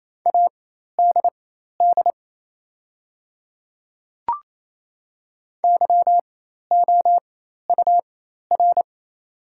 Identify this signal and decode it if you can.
Morse code, heard as a keyed tone, about 27 wpm, spaced as for 13 wpm, 690 Hz. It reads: ADD YOUR
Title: 4x Character Spacing - Top 200 Words - 28wpm